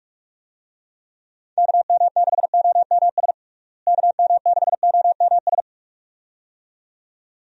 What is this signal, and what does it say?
Morse code, heard as a keyed tone, about 47 wpm, 700 Hz. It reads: KM6OMS KM6OMS
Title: USA Call Signs - 6 Characters - Repeated 2x - 45wpm